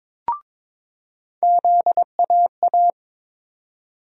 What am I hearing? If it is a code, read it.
Morse code, heard as a keyed tone, about 22 wpm, 700 Hz. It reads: ZAA
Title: USA Call Sign Postfixes - 3 Letters 22wpm